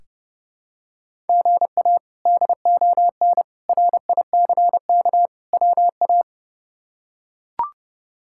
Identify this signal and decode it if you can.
Morse code, heard as a keyed tone, about 30 wpm, 700 Hz. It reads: GA DON RICK WA
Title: K1USN SST QSO Practice - 30wpm